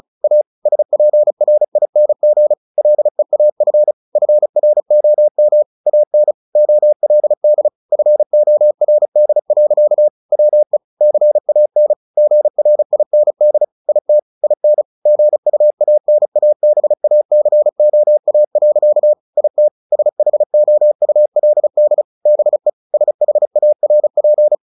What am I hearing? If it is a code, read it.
Morse code, heard as a keyed tone, about 34 wpm, 600 Hz. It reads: A SPRING LEAF FROM AN OLD FORD. WE CAN GRIND IT IN GUANABACOA. IT SHOULD BE SHARP
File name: the old man and the sea 2 35wpm